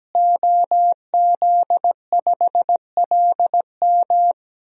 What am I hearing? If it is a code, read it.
Morse code, heard as a keyed tone, about 17 wpm, 700 Hz. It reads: OZ5LM
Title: DXCC-100 Call Signs - 5 Characters - Repeated 3x - 17wpm